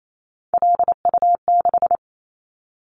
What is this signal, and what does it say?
Morse code, heard as a keyed tone, about 28 wpm, 700 Hz. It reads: LU6
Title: Callsign Prefixes for Top 50 DXCC - 28wpm